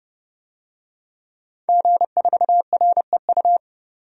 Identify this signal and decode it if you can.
Morse code, heard as a keyed tone, about 30 wpm, 700 Hz. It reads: G4REU
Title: DXCC-100 Call Signs - 5 Characters - 30wpm